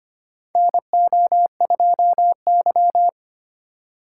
Morse code, heard as a keyed tone, about 25 wpm, 700 Hz. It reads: NO2Y
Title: USA Call Signs - 4 Characters - 2x Character Spacing - 25wpm